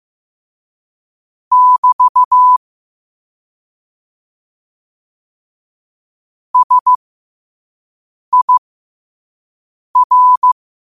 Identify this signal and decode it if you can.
Morse code, heard as a keyed tone, about 15 wpm, spaced as for 5 wpm, 1000 Hz. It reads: <BT> SIR
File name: qotdfiles / 5wpm